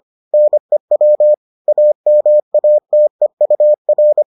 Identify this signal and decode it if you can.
Morse code, heard as a keyed tone, about 25 wpm, 600 Hz. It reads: NEW AMATEUR